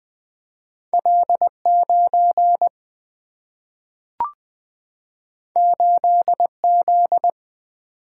Morse code, heard as a keyed tone, about 20 wpm, 700 Hz. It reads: L 9 8 Z
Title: Single-Letter Single-Number 20wpm